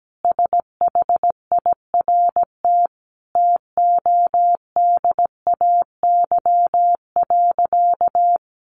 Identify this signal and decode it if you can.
Morse code, heard as a keyed tone, about 17 wpm, 700 Hz. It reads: SHIRT TODAY.
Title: Additional Easy Phrases 17wpm